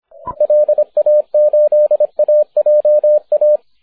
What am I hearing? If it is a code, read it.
Morse code, heard as a keyed tone, about 28 wpm, 560 Hz, with radio noise. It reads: LA8AJA
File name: wp / e4x 6m cw may 30th 2010